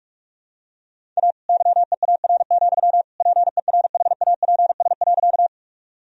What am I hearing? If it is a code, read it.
Morse code, heard as a keyed tone, about 46 wpm, 700 Hz. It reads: A YEAR, PERHAPS.
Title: Sentences from Top 1000 Words - Review Part 1 - 45wpm